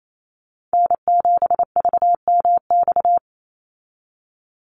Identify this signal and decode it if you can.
Morse code, heard as a keyed tone, about 28 wpm, 700 Hz. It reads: N7VMX
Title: USA Call Signs 28wpm